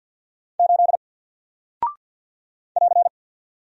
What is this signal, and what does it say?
Morse code, heard as a keyed tone, about 48 wpm, 700 Hz. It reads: 8 <AR>
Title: Single Character (Letter:Number:Punctuation:Prosign) 50wpm